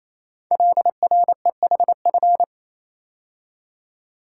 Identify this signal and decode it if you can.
Morse code, heard as a keyed tone, about 27 wpm, 700 Hz. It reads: LREHF
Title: Code Groups - 5 Letters - 28wpm